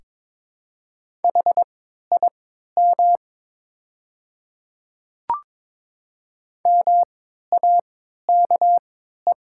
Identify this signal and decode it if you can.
Morse code, heard as a keyed tone, about 22 wpm, spaced as for 12 wpm, 700 Hz. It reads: HIM MAKE
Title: 3x Character Spacing - Top 100 Words - 22wpm